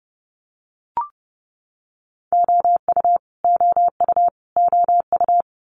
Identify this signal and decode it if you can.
Morse code, heard as a keyed tone, about 30 wpm, 700 Hz. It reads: OU OU OU
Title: Multi-letter Phonemes Repeated 3x 30wpm